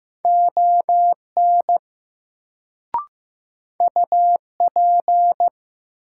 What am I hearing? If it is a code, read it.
Morse code, heard as a keyed tone, about 16 wpm, 700 Hz. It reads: ON UP
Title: Sentences from Top 100 Words - New Word Drill - 15wpm